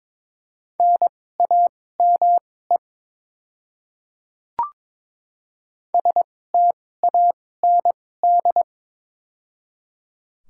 Morse code, heard as a keyed tone, about 22 wpm, spaced as for 16 wpm, 700 Hz. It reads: NAME STAND
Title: 2x Character Spacing - Top 300 Words - 22wpm